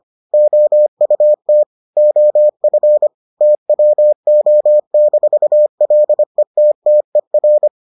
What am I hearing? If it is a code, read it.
Morse code, heard as a keyed tone, about 25 wpm, 600 Hz. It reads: OUT OF TWO-LETTER